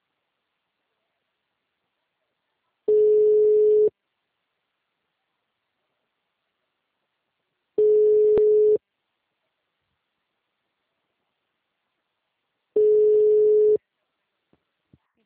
Ringback tone (Europe, 425 Hz).
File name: out- -2007- - - .96201